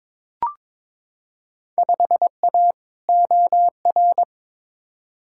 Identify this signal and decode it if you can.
Morse code, heard as a keyed tone, about 22 wpm, 700 Hz. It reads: 5A OR
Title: Field Day Exchanges - 22wpm